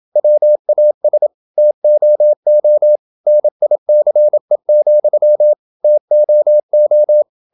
Morse code, heard as a keyed tone, about 27 wpm, 600 Hz. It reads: WAS TOO NICE, TOO
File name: Never 0027wpm